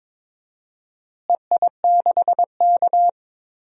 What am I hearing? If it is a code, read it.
Morse code, heard as a keyed tone, about 22 wpm, 700 Hz. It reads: EI6K
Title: DXCC-100 Call Signs - 4 Characters - Repeated 2x - 22wpm